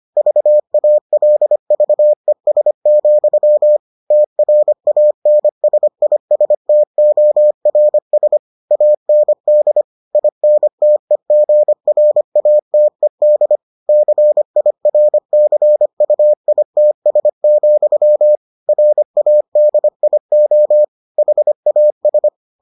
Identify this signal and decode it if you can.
Morse code, heard as a keyed tone, about 25 wpm, 600 Hz. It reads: VALVES, TRANSISTORS AND INTEGRATED CIRCUITS, RADIO HAS